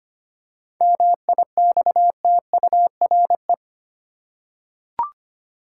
Morse code, heard as a keyed tone, about 25 wpm, 700 Hz. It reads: MIXTURE
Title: Seven Letter Words - 25wpm